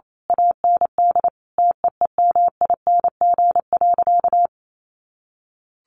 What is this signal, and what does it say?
Morse code, heard as a keyed tone, about 28 wpm, 700 Hz. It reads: AND TEEMING.
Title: A Sound Of Thunder 28wpm